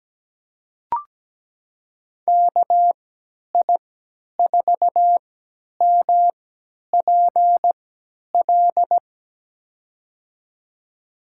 Morse code, heard as a keyed tone, about 17 wpm, spaced as for 10 wpm, 700 Hz. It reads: KI4MPL
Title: USA Call Signs - 6 Characters - 3x Character Spacing - 17wpm